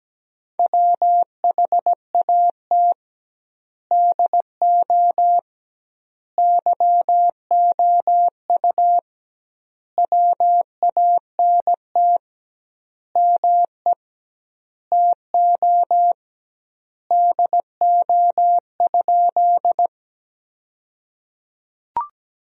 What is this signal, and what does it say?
Morse code, heard as a keyed tone, about 17 wpm, 700 Hz. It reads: WHAT DO YOU WANT ME TO DO?
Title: 2x Word-Spacing - Sentences from Top 200 Words - 17wpm